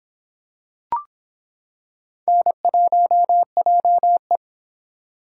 Morse code, heard as a keyed tone, about 26 wpm, 700 Hz. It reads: N1JE